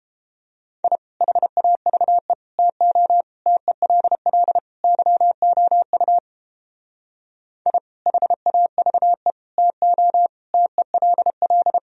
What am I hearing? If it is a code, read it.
Morse code, heard as a keyed tone, about 29 wpm, 700 Hz. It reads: I HAVE TO TELL YOU I HAVE TO TELL